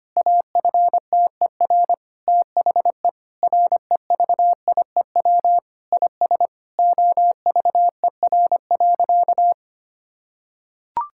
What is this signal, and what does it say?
Morse code, heard as a keyed tone, about 25 wpm, 700 Hz. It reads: AFTER THE REVIEW IS OVER.
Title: Sentences from Top 1000 Words - Review Part 6 - 25wpm